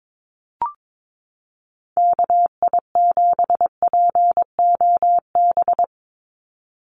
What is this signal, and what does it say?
Morse code, heard as a keyed tone, about 22 wpm, 700 Hz. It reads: KI7POB
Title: USA Call Signs 22wpm